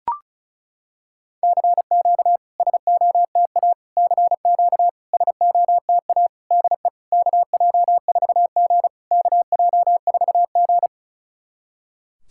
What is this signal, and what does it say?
Morse code, heard as a keyed tone, about 36 wpm, 700 Hz. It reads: CQ SOTA CQ SOTA DE KJ4G KJ4G
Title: SOTA QSO Practice 35wpm